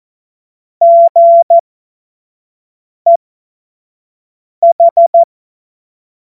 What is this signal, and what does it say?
Morse code, heard as a keyed tone, about 14 wpm, spaced as for 5 wpm, 680 Hz. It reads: GEH
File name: Morsekurs / Witze / Witz 10. Mai 2022 25 F70